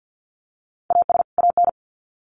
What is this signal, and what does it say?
Morse code, heard as a keyed tone, about 50 wpm, 700 Hz. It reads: AS AN
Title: Sets of 2 Words 50wpm